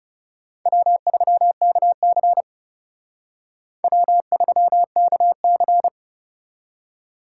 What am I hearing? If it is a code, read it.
Morse code, heard as a keyed tone, about 38 wpm, 700 Hz. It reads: W3KC W3KC